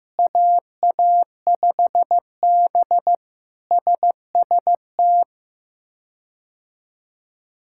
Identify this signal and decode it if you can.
Morse code, heard as a keyed tone, about 15 wpm, 700 Hz. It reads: AA5B SST
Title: K1USN SST QSO Practice - 15wpm